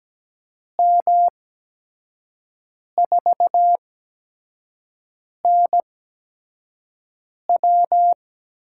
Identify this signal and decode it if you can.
Morse code, heard as a keyed tone, about 17 wpm, spaced as for 5 wpm, 700 Hz. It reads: M4NW
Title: USA Call Signs - 5 Characters - 8x Character Spacing - 17wpm